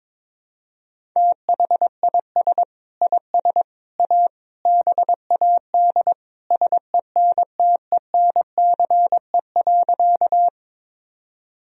Morse code, heard as a keyed tone, about 22 wpm, 700 Hz. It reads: THIS IS A BAD SENTENCE.
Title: Sentences from Top 1000 Words - Review Part 1 - 22wpm